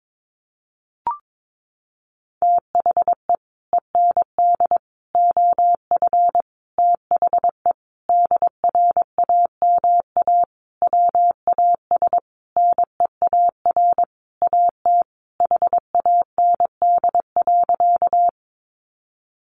Morse code, heard as a keyed tone, about 22 wpm, 700 Hz. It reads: THE END OF THE DRAMA WAS NEAR AT HAND.
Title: Sentences from Top 1500 Words 22wpm